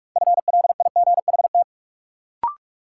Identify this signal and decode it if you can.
Morse code, heard as a keyed tone, about 46 wpm, 700 Hz. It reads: FLIGHT
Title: Sentences from Top 1000 Words - New Word Drill - 45wpm